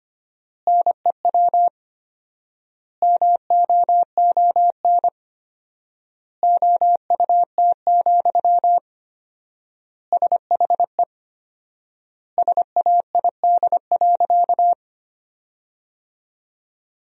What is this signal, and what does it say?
Morse code, heard as a keyed tone, about 25 wpm, 700 Hz. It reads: NEW MOON OUT, SHE SAID.
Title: 4x Word-Spacing - Sentences from Top 500 Words - 25wpm